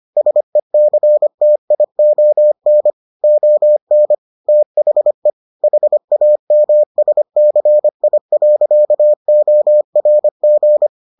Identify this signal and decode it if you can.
Morse code, heard as a keyed tone, about 25 wpm, 600 Hz. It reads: SECTION ON THE HAMSCI.ORG